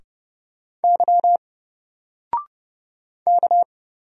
Morse code, heard as a keyed tone, about 30 wpm, 700 Hz. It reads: Y K
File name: Single-Letter-Number-30wpm